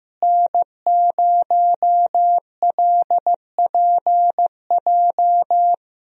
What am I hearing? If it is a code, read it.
Morse code, heard as a keyed tone, about 15 wpm, 700 Hz. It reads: N0LPJ
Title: USA Call Signs - 5 Characters - Repeated 2x - 15wpm